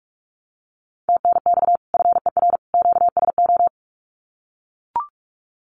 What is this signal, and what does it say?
Morse code, heard as a keyed tone, about 44 wpm, 700 Hz. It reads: TNX FER QSO
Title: Common QSO Phrases 45wpm